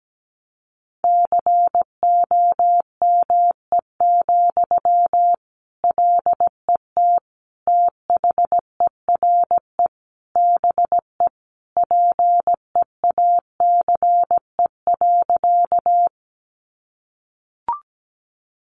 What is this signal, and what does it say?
Morse code, heard as a keyed tone, about 17 wpm, 700 Hz. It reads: COME, LET THERE BE PEACE.
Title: Sentences from Top 1500 Words 17wpm